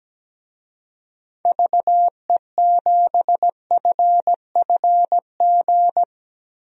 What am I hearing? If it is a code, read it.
Morse code, heard as a keyed tone, about 17 wpm, 700 Hz. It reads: VE7FFG
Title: DXCC-100 Call Signs - 6 Characters - Repeated 2x - 17wpm